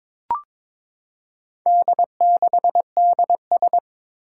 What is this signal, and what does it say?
Morse code, heard as a keyed tone, about 22 wpm, 700 Hz. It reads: D6DS
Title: Four Challenging Letter-Number Combinations 22wpm